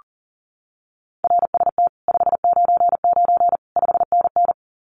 Morse code, heard as a keyed tone, about 38 wpm, 700 Hz. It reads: RST 599 5NN
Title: Common QSO Phrases Encore 40wpm